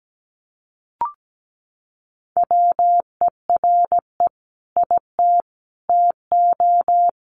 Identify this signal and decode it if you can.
Morse code, heard as a keyed tone, about 17 wpm, 700 Hz. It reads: WERE IT TO
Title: Sentences from Top 1300 Words 17wpm